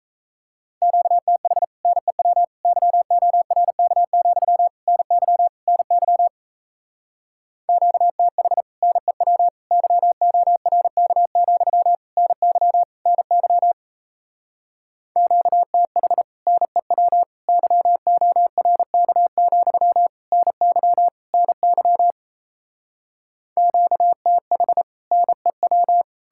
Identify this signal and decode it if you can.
Morse code, heard as a keyed tone, about 37 wpm, 700 Hz. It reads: QTH NEW YORK, NY NY QTH NEW YORK, NY NY QTH NEW YORK, NY NY QTH NEW